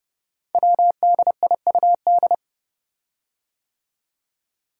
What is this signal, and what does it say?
Morse code, heard as a keyed tone, about 30 wpm, 700 Hz. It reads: WDIUD
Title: Code Groups - 5 Letter:Numbers - 30wpm